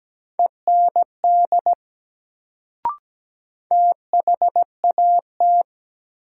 Morse code, heard as a keyed tone, about 17 wpm, 700 Hz. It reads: END THAT
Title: Sets of 2 Words - New Word Drill - 17wpm